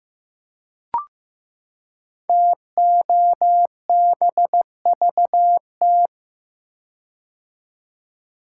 Morse code, heard as a keyed tone, about 15 wpm, 700 Hz. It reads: TOBVT
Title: Code Groups - 5 Letters - 15wpm